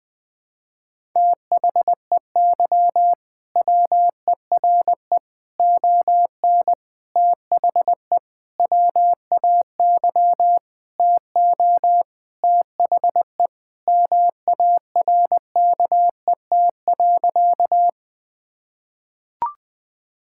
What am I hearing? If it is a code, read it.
Morse code, heard as a keyed tone, about 20 wpm, 700 Hz. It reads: THEY WERE ON THE WAY TO THE MARKET.
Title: Sentences with 5-Word Phrases - 20wpm